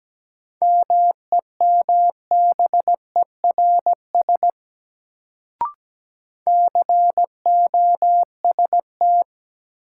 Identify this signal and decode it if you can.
Morse code, heard as a keyed tone, about 17 wpm, 700 Hz. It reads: MEMBERS COST
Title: Sets of 3 Words - New Word Drill - 17wpm